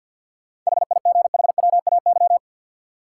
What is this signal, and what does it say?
Morse code, heard as a keyed tone, about 50 wpm, 700 Hz. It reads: HIGHWAY
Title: Sentences from Top 1400 Words - New Word Drill - 50wpm